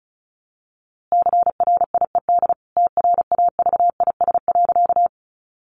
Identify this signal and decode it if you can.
Morse code, heard as a keyed tone, about 34 wpm, 700 Hz. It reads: CRIED TRAVIS.
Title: A Sound Of Thunder 35wpm